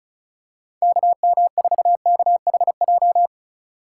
Morse code, heard as a keyed tone, about 36 wpm, 700 Hz. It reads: KM4KHJ
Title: USA Call Signs - 6 Characters - 2x Character Spacing - 35wpm